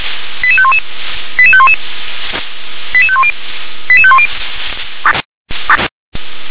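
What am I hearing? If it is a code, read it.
Five-tone selective call (ZVEI), heard as quick runs of five tones, tones of about 70 ms. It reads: 88511 88511 88510 88510
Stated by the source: ZVEI five-tone call received on an SDR.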